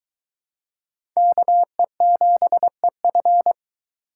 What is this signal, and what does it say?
Morse code, heard as a keyed tone, about 22 wpm, 700 Hz. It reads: KE7EF